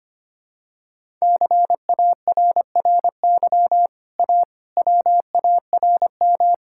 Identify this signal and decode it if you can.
Morse code, heard as a keyed tone, about 25 wpm, 700 Hz. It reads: CARRY A WARM
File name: Easy Sentences with Hints - 25wpm